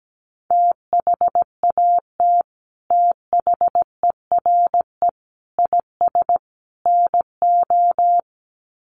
Morse code, heard as a keyed tone, about 17 wpm, 700 Hz. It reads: THAT THERE IS NO
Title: Sets of 4 Words 17wpm